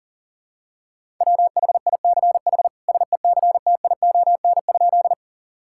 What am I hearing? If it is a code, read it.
Morse code, heard as a keyed tone, about 40 wpm, 700 Hz. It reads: WHICH SECTION?
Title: Sentences from Top 1000 Words - Review Part 5 - 40wpm